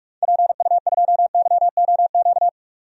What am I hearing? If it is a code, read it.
Morse code, heard as a keyed tone, about 47 wpm, 700 Hz. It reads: PU2YOQ
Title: DXCC-100 Call Signs - 6 Characters - 45wpm